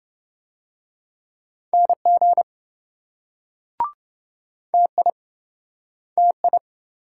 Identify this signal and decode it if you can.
Morse code, heard as a keyed tone, about 28 wpm, 700 Hz. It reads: NG TI TI